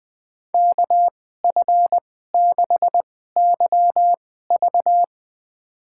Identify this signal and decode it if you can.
Morse code, heard as a keyed tone, about 20 wpm, spaced as for 15 wpm, 700 Hz. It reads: KF6YV
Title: USA Call Signs - 2x Character Spacing - 20wpm